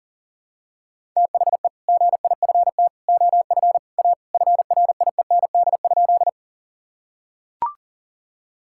Morse code, heard as a keyed tone, about 40 wpm, 700 Hz. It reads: THE GIFT OF A FRIEND?
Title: Sentences from Top 1000 Words - Review Part 6 - 40wpm